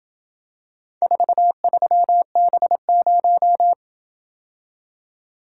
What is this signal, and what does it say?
Morse code, heard as a keyed tone, about 26 wpm, 700 Hz. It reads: 43B0